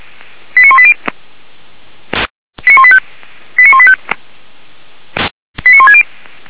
Five-tone selective call (ZVEI), heard as quick runs of five tones, tones of about 70 ms. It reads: 89180 89185 89185 89160